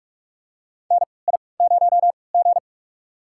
Morse code, heard as a keyed tone, about 42 wpm, spaced as for 25 wpm, 700 Hz. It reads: NI0G
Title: USA Call Signs - 4 Characters - 3x Character Spacing - 45wpm